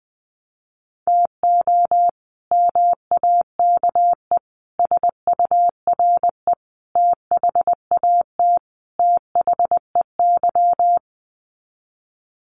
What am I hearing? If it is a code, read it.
Morse code, heard as a keyed tone, about 20 wpm, 700 Hz. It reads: TO MAKE SURE THAT THEY
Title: Sets of 5 Words Encore2 20wpm